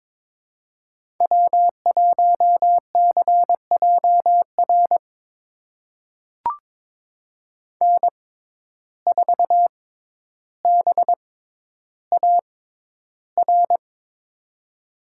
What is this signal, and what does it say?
Morse code, heard as a keyed tone, about 21 wpm, spaced as for 11 wpm, 700 Hz. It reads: W1CJR N4BAR
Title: USA Call Signs - 6x Character Spacing - 22wpm